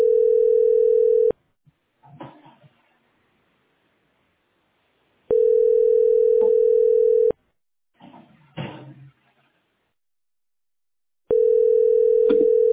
Ringback tone (North America), 440 + 480 Hz.